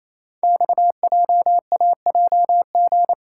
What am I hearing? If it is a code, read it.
Morse code, heard as a keyed tone, about 28 wpm, 700 Hz. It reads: XJAJG
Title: Code Groups - 5 Letters - 28wpm